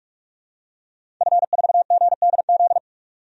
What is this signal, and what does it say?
Morse code, heard as a keyed tone, about 46 wpm, 700 Hz. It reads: F4GDZ
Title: DXCC-100 Call Signs - 5 Characters - 45wpm